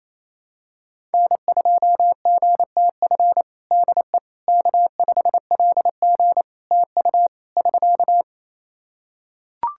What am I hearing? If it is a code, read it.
Morse code, heard as a keyed tone, about 28 wpm, 700 Hz. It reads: N2GTF DE K5LG TU <SK>
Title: Practice QSO - Exchange 4 - Ending - 28wpm